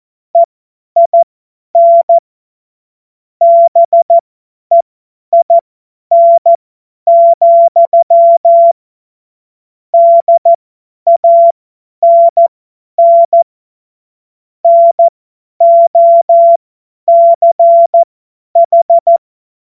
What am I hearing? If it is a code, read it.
Morse code, heard as a keyed tone, about 14 wpm, spaced as for 10 wpm, 680 Hz. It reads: EIN BEIN, DANN NOCH
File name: Morsekurs / Witze / Witz 01. August 2023 50 F70